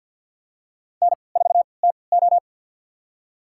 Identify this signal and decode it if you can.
Morse code, heard as a keyed tone, about 46 wpm, spaced as for 27 wpm, 700 Hz. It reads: N4TO
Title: USA Call Signs - 4 Characters - 3x Character Spacing - 50wpm